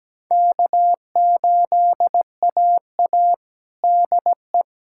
Morse code, heard as a keyed tone, about 17 wpm, 700 Hz. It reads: K8AA DE
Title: USA Callsign DE Callsign - 4 Characters - 17wpm